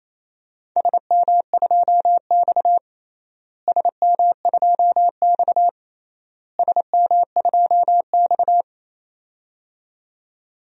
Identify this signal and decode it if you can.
Morse code, heard as a keyed tone, about 28 wpm, 700 Hz. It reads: SM2X SM2X SM2X
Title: DXCC-100 Call Signs - 4 Characters - Repeated 3x - 28wpm